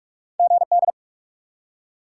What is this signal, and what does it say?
Morse code, heard as a keyed tone, about 46 wpm, 700 Hz. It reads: GD
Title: Top 90 QSO Elements 45wpm